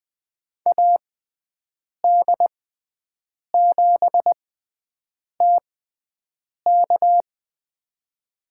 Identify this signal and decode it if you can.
Morse code, heard as a keyed tone, about 20 wpm, spaced as for 7 wpm, 700 Hz. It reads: AD7TK
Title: USA Call Signs - 5 Characters - 6x Character Spacing - 20wpm